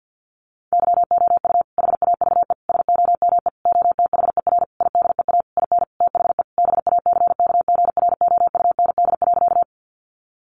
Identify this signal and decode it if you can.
Morse code, heard as a keyed tone, about 50 wpm, 700 Hz. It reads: YOU HAVE SOME OTHER IDEA IN THE BACKGROUND.
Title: Sentences from Top 1300 Words 50wpm